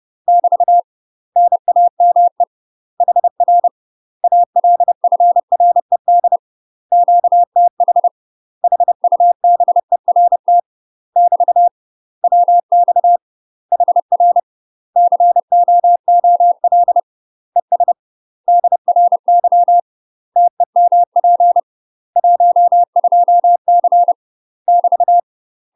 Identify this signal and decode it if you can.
Morse code, heard as a keyed tone, about 30 wpm, 700 Hz. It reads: <BT> NAME HR ALFRED QTH HUBERT <BT> WX HR COOL ES DRY TEMP 12C <BT>